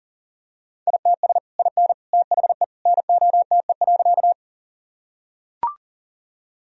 Morse code, heard as a keyed tone, about 40 wpm, 700 Hz. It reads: ITS IN THE NOTE.
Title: Sentences from Top 1000 Words - Review Part 2 - 40wpm